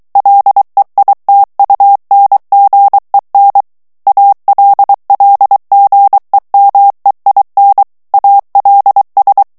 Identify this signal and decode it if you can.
Morse code, heard as a keyed tone, about 23 wpm, 800 Hz. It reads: LEITUNGEN ALLGEMEIN ALS
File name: morse mp3 / ALLEMAND texte clair 1 vitesse 22 mots minute